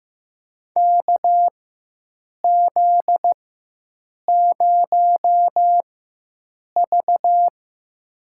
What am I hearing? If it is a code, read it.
Morse code, heard as a keyed tone, about 15 wpm, spaced as for 7 wpm, 700 Hz. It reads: KZ0V
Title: USA Call Signs - 4 Characters - 4x Character Spacing - 15wpm